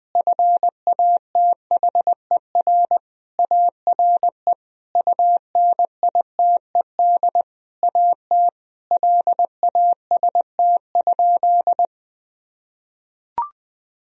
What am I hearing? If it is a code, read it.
Morse code, heard as a keyed tone, about 20 wpm, 700 Hz. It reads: FATHER ARE UNITED AT LAST?
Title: Sentences from Top 1000 Words - Review Part 4 - 20wpm